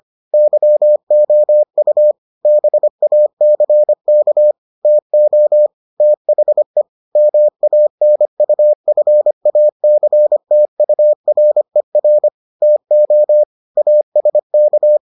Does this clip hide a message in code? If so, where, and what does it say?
Yes: Morse code from 0.3 to 15.0 s, heard as a keyed tone, about 25 wpm, 600 Hz. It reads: YOU BACK TO THE MANUFACTURER TO ASK